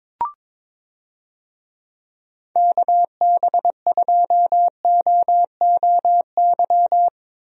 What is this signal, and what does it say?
Morse code, heard as a keyed tone, about 22 wpm, 700 Hz. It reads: KB2OOY
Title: USA Call Signs - 6 Characters - Repeated 2x - 22wpm